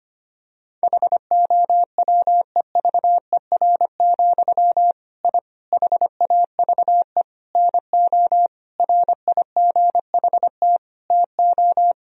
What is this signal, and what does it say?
Morse code, heard as a keyed tone, about 25 wpm, 700 Hz. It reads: HOWEVER, I HAVE NO RIGHT TO
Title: Sentences from Top 1000 Words - Review Part 3 - 25wpm